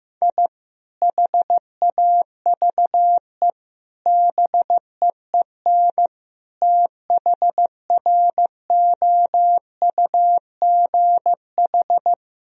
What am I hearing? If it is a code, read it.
Morse code, heard as a keyed tone, about 15 wpm, 700 Hz. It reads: I HAVE BEEN THROUGH
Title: Sentences from Top 1000 Words - Review Part 1 - 15wpm